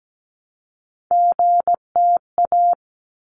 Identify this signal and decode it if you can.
Morse code, heard as a keyed tone, about 17 wpm, 700 Hz. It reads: GTA
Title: ARRL Sections 17wpm